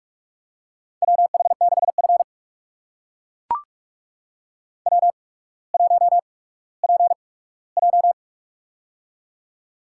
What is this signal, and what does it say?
Morse code, heard as a keyed tone, about 44 wpm, spaced as for 19 wpm, 700 Hz. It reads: WH6F W1PJ
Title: USA Call Signs - 4 Characters - 8x Character Spacing - 45wpm